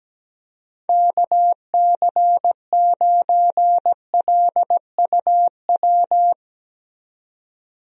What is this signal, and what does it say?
Morse code, heard as a keyed tone, about 17 wpm, 700 Hz. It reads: KC9LUW
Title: USA Call Signs - 6 Characters - 17wpm